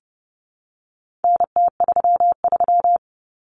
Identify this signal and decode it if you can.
Morse code, heard as a keyed tone, about 30 wpm, 700 Hz. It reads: NT33
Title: Common Cut Numbers - 4-Digit 30wpm